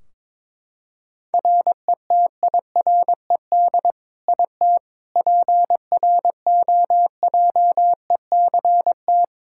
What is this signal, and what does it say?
Morse code, heard as a keyed tone, about 22 wpm, 700 Hz. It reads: RETIRED IT PROJECT
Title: Practice QSO - Exchange 3 - More Info - 22wpm